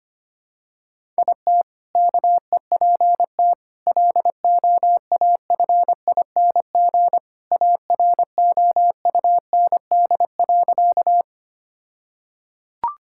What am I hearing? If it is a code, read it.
Morse code, heard as a keyed tone, about 25 wpm, 700 Hz. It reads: IT KEPT LOAFING AROUND.
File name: practice / 25wpm / All-Age Appropriate Jokes - 25wpm